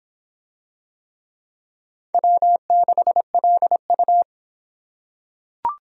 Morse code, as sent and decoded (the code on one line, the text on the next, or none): .-- -.... .-.. ..-
W6LU